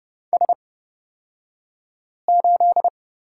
Morse code, heard as a keyed tone, about 29 wpm, spaced as for 5 wpm, 700 Hz. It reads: S8
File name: Lesson 37d - Letter x - Two Characters 30wpm